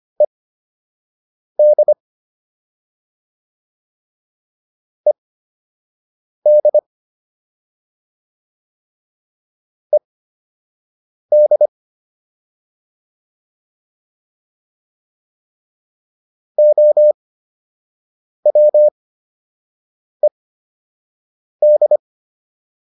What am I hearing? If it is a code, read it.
Morse code, heard as a keyed tone, about 25 wpm, spaced as for 6 wpm, 600 Hz. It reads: ED ED ED OWED